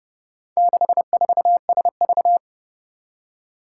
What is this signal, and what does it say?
Morse code, heard as a keyed tone, about 30 wpm, 700 Hz. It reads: 64SV
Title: Four Challenging Letter-Number Combinations 30wpm